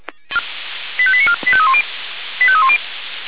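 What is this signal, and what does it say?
Five-tone selective call (ZVEI), heard as quick runs of five tones, tones of about 70 ms. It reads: 85894 85310 85310
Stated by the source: ZVEI five-tone call 85312 received by SDR.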